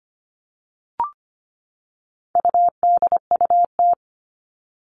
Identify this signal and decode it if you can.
Morse code, heard as a keyed tone, about 25 wpm, 700 Hz. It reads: UDUT
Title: All Cut Numbers - 4-Digit 25wpm